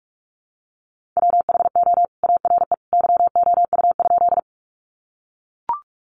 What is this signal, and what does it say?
Morse code, heard as a keyed tone, about 44 wpm, 700 Hz. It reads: WHO ARE YOU?
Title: Sentences from Top 100 Words 45wpm